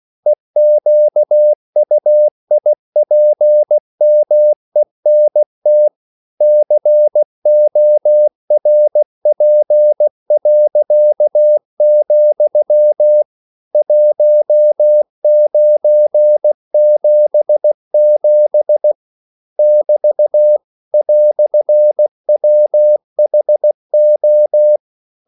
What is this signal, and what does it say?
Morse code, heard as a keyed tone, about 16 wpm, 600 Hz. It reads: EQUIPMENT CORP., 1977 <BT> "WHO
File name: CW / Practice / EnglishQuotations0416